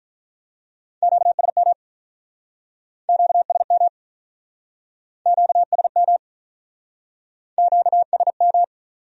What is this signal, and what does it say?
Morse code, heard as a keyed tone, about 48 wpm, 700 Hz. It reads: QSM QSM QSM QSM